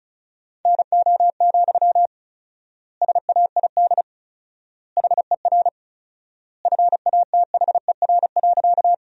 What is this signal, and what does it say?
Morse code, heard as a keyed tone, about 36 wpm, 700 Hz. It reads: NO, SAID HER FATHER.
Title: 4x Word-Spacing - Sentences from Top 300 Words - 35wpm